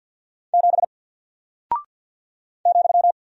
Morse code, heard as a keyed tone, about 48 wpm, 700 Hz. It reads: 7 ,
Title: Single Character (Letter:Number:Punctuation:Prosign) Alt 50wpm